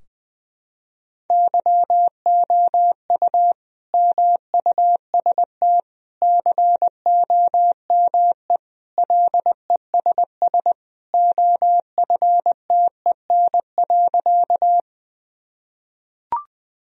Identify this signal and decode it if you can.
Morse code, heard as a keyed tone, about 21 wpm, 700 Hz. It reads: YOU MUST COME LESS OFTEN.
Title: Sentences from Top 1000 Words - Review Part 2 - 20wpm